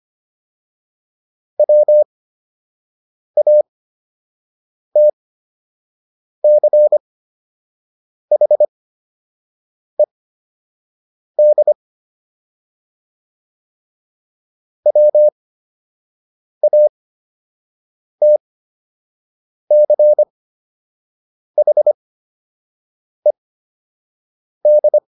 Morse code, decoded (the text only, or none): WATCHED WATCHED